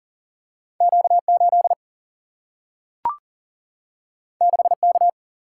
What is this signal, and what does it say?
Morse code, heard as a keyed tone, about 39 wpm, 700 Hz. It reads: Q 8 6 K
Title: Single-Letter Single-Number 40wpm